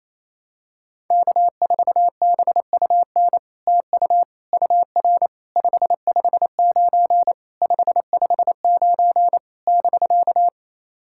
Morse code, heard as a keyed tone, about 29 wpm, 700 Hz. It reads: K4BUN TU UR 559 559 <BK>
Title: SOTA QSO Practice 28wpm